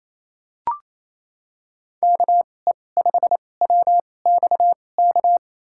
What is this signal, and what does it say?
Morse code, heard as a keyed tone, about 28 wpm, spaced as for 20 wpm, 700 Hz. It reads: KE5WXK
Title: USA Call Signs - 6 Characters - 2x Character Spacing - 28wpm